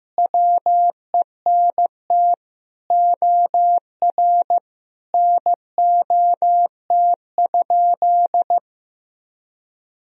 Morse code, heard as a keyed tone, about 15 wpm, 700 Hz. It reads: WENT OR NOT?
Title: Sentences from Top 1000 Words - Review Part 3 - 15wpm